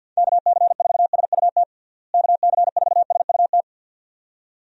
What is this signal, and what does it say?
Morse code, heard as a keyed tone, about 52 wpm, 700 Hz. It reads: KC4SUT KC4SUT
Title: USA Call Signs - 6 Characters - Repeated 2x - 50wpm